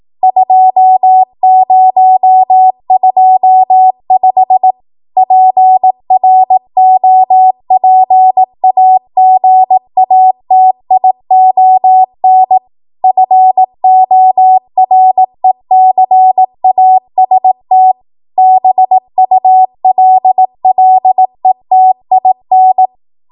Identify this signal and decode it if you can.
Morse code, heard as a keyed tone, about 18 wpm, 750 Hz. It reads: 2025 PROPAGATION FORECAST BULLETIN
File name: Morse / 18WPM